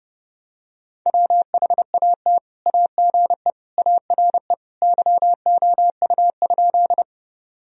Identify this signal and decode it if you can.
Morse code, heard as a keyed tone, about 32 wpm, 700 Hz. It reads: WHAT AGE ARE YOU?
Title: Sentences from Top 1000 Words - Review Part 3 - 30wpm